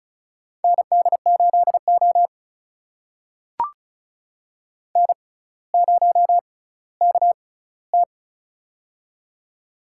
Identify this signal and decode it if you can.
Morse code, heard as a keyed tone, about 36 wpm, spaced as for 18 wpm, 700 Hz. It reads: ND8O N0KT
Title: USA Call Signs - 4 Characters - 6x Character Spacing - 35wpm